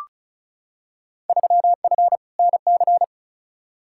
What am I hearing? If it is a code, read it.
Morse code, heard as a keyed tone, about 36 wpm, 700 Hz. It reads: 3F NC
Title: Field Day Exchanges - 35wpm